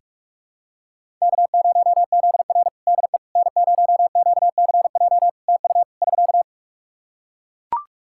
Morse code, heard as a keyed tone, about 41 wpm, 700 Hz. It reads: K0ZR DE N0QCJ TU <SK>
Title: Practice QSO - Exchange 4 - Ending - 45wpm